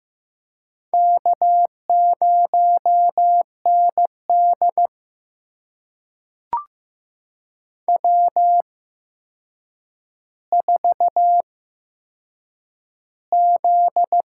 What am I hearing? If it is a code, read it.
Morse code, heard as a keyed tone, about 15 wpm, spaced as for 7 wpm, 700 Hz. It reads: K0ND W4Z
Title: USA Call Signs - 4 Characters - 8x Character Spacing - 15wpm